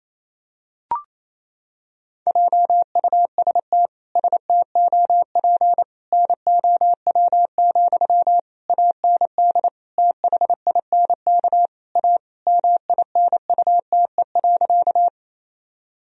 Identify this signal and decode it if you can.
Morse code, heard as a keyed tone, about 28 wpm, 700 Hz. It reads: JUST STOP NOW, AND THINK A MINUTE.
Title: Sentences from Top 1000 Words - Review Part 1 - 28wpm